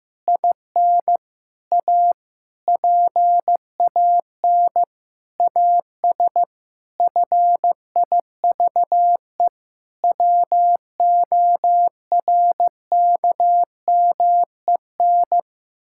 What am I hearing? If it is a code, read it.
Morse code, heard as a keyed tone, about 15 wpm, 700 Hz. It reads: IN A PAN AS FIVE WORKMEN
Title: Pangram Challenge v2 - 15wpm